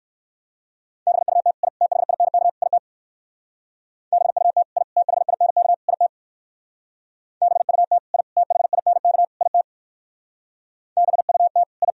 Morse code, heard as a keyed tone, about 58 wpm, 700 Hz. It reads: BUT I THINK IT BUT I THINK IT BUT I THINK IT BUT I